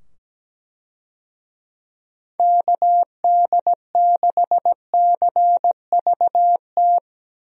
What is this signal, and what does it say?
Morse code, heard as a keyed tone, about 17 wpm, 700 Hz. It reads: KD6CVT
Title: USA Call Signs - 6 Characters - 17wpm